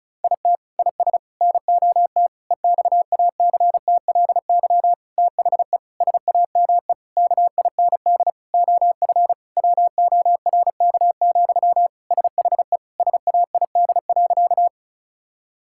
Morse code, heard as a keyed tone, about 36 wpm, 700 Hz. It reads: IT IS NOT EXACTLY THE SAME KIND OF WORK, SHE SAID.
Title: Sentences from Top 1000 Words - Review Part 6 - 35wpm